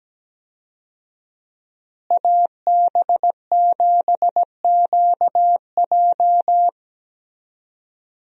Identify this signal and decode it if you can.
Morse code, heard as a keyed tone, about 17 wpm, 700 Hz. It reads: AB7QJ
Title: USA Call Signs - 5 Characters - Repeated 2x - 17wpm